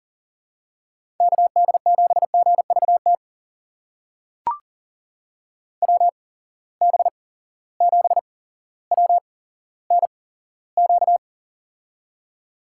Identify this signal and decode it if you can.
Morse code, heard as a keyed tone, about 40 wpm, spaced as for 17 wpm, 700 Hz. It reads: KD7GVT WB7WNQ
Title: USA Call Signs - 6 Characters - 8x Character Spacing - 40wpm